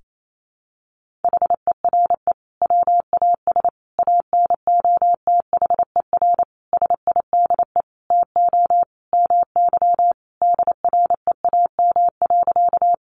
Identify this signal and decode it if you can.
Morse code, heard as a keyed tone, about 28 wpm, 700 Hz. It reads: HERE WAS ANOTHER SIDE TO MY DREAM.
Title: Sentences from Top 1500 Words 28wpm